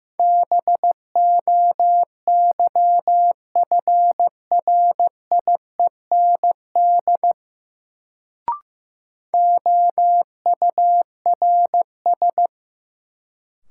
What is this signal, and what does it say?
Morse code, heard as a keyed tone, about 15 wpm, 700 Hz. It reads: BOYFRIEND OURS
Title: Sentences from Top 1000 Words - Review Part 5 - New Word Drill - 15wpm